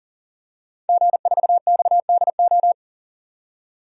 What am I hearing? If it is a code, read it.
Morse code, heard as a keyed tone, about 38 wpm, 700 Hz. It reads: G4XDO
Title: DXCC-100 Call Signs - 5 Characters - 40wpm